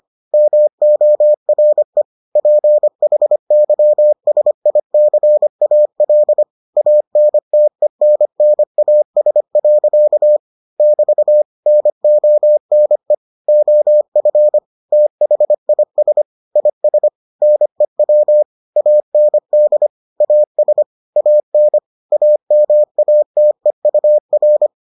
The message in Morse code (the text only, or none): MORE PHYSICAL ANTENNAS. <BT> NONE OF THIS IS NEW AND AS AN AMATEUR